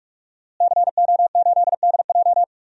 Morse code, heard as a keyed tone, about 43 wpm, 700 Hz. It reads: CO8DJ
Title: DXCC-100 Call Signs - 5 Characters - Repeated 3x - 45wpm